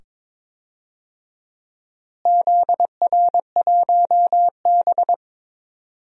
Morse code, heard as a keyed tone, about 22 wpm, 700 Hz. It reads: ZR1B
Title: DXCC-100 Call Signs - 4 Characters - Repeated 2x - 22wpm